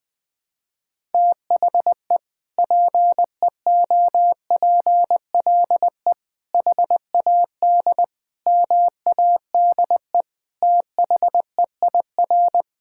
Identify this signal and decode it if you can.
Morse code, heard as a keyed tone, about 21 wpm, 700 Hz. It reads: THE PEOPLE HAD MADE THEIR
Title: Sentences from Top 1000 Words - Review Part 6 - 20wpm